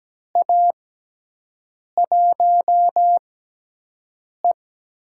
Morse code, heard as a keyed tone, about 17 wpm, spaced as for 6 wpm, 700 Hz. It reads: A1E
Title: USA Call Signs - 6x Character Spacing - 17wpm